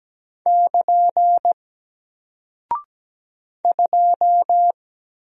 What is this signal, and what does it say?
Morse code, heard as a keyed tone, about 17 wpm, 700 Hz. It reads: <KN> 2
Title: Single Character (Letter:Number:Punctuation:Prosign) 17wpm